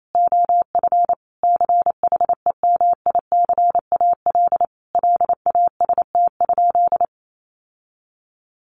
Morse code, heard as a keyed tone, about 28 wpm, 700 Hz. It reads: OF CHEMICAL LAST?
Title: Sentences from Top 1200 Words 28wpm